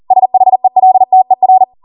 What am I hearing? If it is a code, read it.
Morse code, heard as a keyed tone, about 40 wpm, 750 Hz. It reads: SHELTER